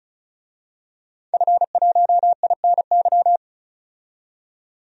Morse code, heard as a keyed tone, about 36 wpm, 700 Hz. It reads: F1INY
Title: DXCC-100 Call Signs - 5 Characters - 35wpm